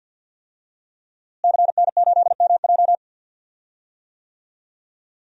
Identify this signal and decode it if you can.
Morse code, heard as a keyed tone, about 52 wpm, 700 Hz. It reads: CN8MJ
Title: DXCC-100 Call Signs - 5 Characters - Repeated 2x - 50wpm